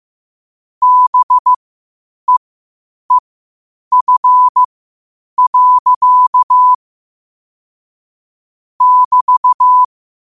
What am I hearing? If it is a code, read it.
Morse code, heard as a keyed tone, about 15 wpm, spaced as for 8 wpm, 1000 Hz. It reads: BEEF. <BT>
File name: qotdfiles / 8wpm